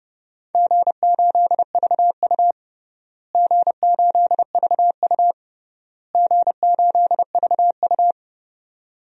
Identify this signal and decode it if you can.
Morse code, heard as a keyed tone, about 30 wpm, 700 Hz. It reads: G8VU G8VU G8VU
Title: DXCC-100 Call Signs - 4 Characters - Repeated 3x - 30wpm